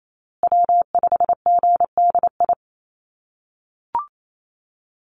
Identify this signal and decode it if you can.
Morse code, heard as a keyed tone, about 28 wpm, 700 Hz. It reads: W5GDI
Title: USA Call Signs Repeated 2x 28wpm